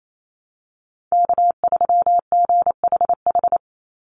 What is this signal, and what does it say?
Morse code, heard as a keyed tone, about 28 wpm, 700 Hz. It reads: K3GHH
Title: Call Signs - 28wpm